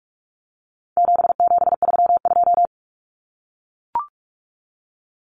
Morse code, heard as a keyed tone, about 44 wpm, 700 Hz. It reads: 7732
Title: Common Cut Numbers - 4-Digit 45wpm